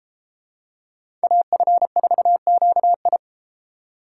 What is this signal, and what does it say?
Morse code, heard as a keyed tone, about 30 wpm, 700 Hz. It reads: AF4QI